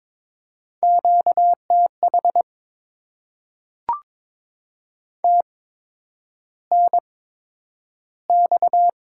Morse code, heard as a keyed tone, about 22 wpm, spaced as for 9 wpm, 700 Hz. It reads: QTH TNX
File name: Top 20 QSO Elements - 8x Character Spacing - 22wpm